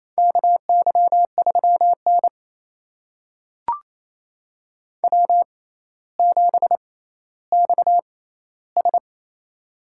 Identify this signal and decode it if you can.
Morse code, heard as a keyed tone, about 28 wpm, spaced as for 14 wpm, 700 Hz. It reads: KY3N W7XS
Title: USA Call Signs - 4 Characters - 6x Character Spacing - 28wpm